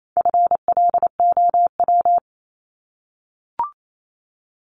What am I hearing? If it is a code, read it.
Morse code, heard as a keyed tone, about 28 wpm, 700 Hz. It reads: FLOW
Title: Top 1001-1250 Words 28wpm